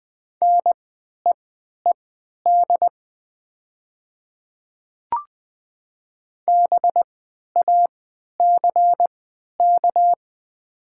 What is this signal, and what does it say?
Morse code, heard as a keyed tone, about 20 wpm, spaced as for 11 wpm, 700 Hz. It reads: NEED BACK
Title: 3x Character Spacing - Top 300 Words - 20wpm